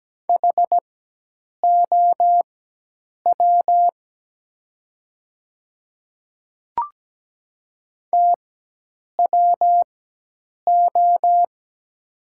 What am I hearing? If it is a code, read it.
Morse code, heard as a keyed tone, about 17 wpm, spaced as for 8 wpm, 690 Hz. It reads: HOW TWO
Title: 4x Character Spacing - Top 100 Words - 17wpm